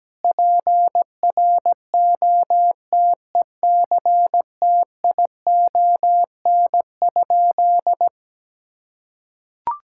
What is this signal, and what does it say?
Morse code, heard as a keyed tone, about 17 wpm, 700 Hz. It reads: PROTECTION?
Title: Sentences from Top 1600 Words 17wpm